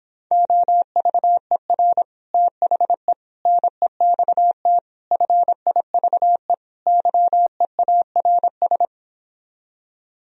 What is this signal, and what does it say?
Morse code, heard as a keyed tone, about 29 wpm, 700 Hz. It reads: OVER THE NEXT FIVE YEARS